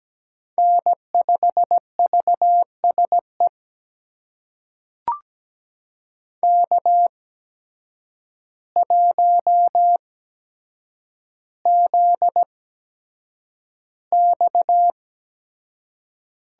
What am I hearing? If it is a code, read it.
Morse code, heard as a keyed tone, about 17 wpm, spaced as for 8 wpm, 700 Hz. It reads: N5VSE K1ZX
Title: USA Call Signs - 5 Characters - 8x Character Spacing - 17wpm